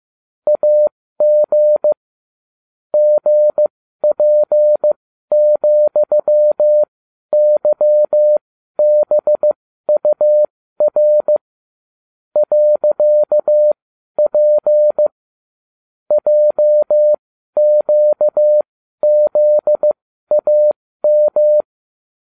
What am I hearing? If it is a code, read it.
Morse code, heard as a keyed tone, about 15 wpm, spaced as for 12 wpm, 600 Hz. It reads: AG GP,YBUR .P JQZAM